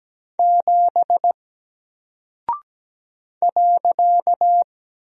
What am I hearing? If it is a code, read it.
Morse code, heard as a keyed tone, about 17 wpm, 700 Hz. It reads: 7 .
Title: Single Character (Letter:Number:Punctuation:Prosign) Alt 17wpm